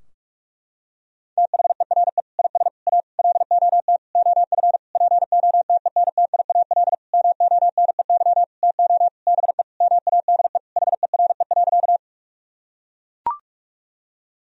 Morse code, heard as a keyed tone, about 44 wpm, 700 Hz. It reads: THERE IS A LOT OF POTENTIAL MONEY TO BE MADE HERE.
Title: Sentences from Top 1000 Words - Review Part 4 - 45wpm